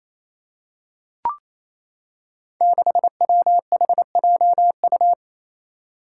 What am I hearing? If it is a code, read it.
Morse code, heard as a keyed tone, about 29 wpm, 700 Hz. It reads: 6WHJU
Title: Five Challenging Letter-Number Combinations 28wpm